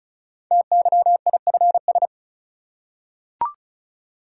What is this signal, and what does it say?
Morse code, heard as a keyed tone, about 36 wpm, 700 Hz. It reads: TYIFS
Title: Code Groups - 5 Letter:Numbers - 35wpm